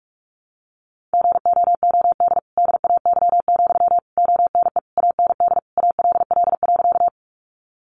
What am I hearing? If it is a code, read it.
Morse code, heard as a keyed tone, about 44 wpm, 700 Hz. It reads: GOOD DAY, ONE AND ALL.
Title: Sentences from Top 200 Words 45wpm